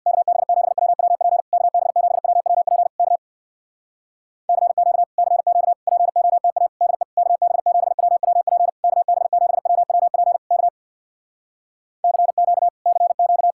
Morse code, heard as a keyed tone, about 66 wpm, 700 Hz. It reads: KB7WWP KB7WWP K CQ CQ POTA DE KB7WWP KB7WWP K CQ CQ